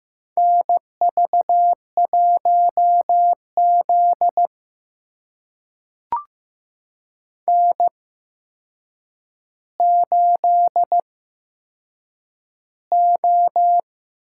Morse code, heard as a keyed tone, about 15 wpm, spaced as for 7 wpm, 700 Hz. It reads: NV1Z N8O
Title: USA Call Signs - 4 Characters - 8x Character Spacing - 15wpm